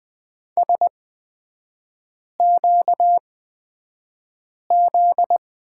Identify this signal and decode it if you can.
Morse code, heard as a keyed tone, about 20 wpm, spaced as for 5 wpm, 700 Hz. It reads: SQZ
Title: 243 Lesson 39e - Letter z - Three Characters 20wpm